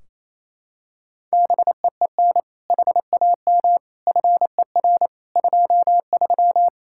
Morse code, heard as a keyed tone, about 28 wpm, 700 Hz. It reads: BEEN HAM FER 23
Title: Practice QSO - Exchange 3 - More Info - 28wpm